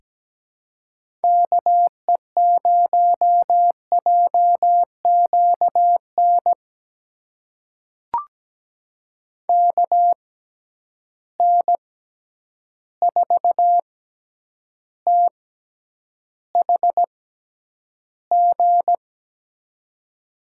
Morse code, heard as a keyed tone, about 17 wpm, spaced as for 9 wpm, 700 Hz. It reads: KE0JQN KN4THG
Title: USA Call Signs - 6 Characters - 6x Character Spacing - 17wpm